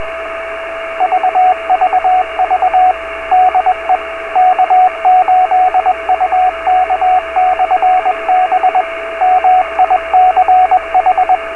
Morse code, heard as a keyed tone, about 21 wpm, 740 Hz, with radio noise. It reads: VVV DE K8UK/B MICH
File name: beacons / k8uk